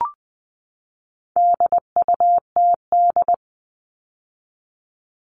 Morse code, heard as a keyed tone, about 20 wpm, 700 Hz. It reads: DUTD
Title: All Cut Numbers - 4-Digit 20wpm